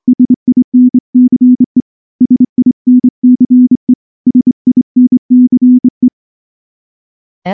Morse code, heard as a keyed tone, about 25 wpm, 250 Hz. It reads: SINCE SINCE SINCE
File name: morse / morse-training-25WPM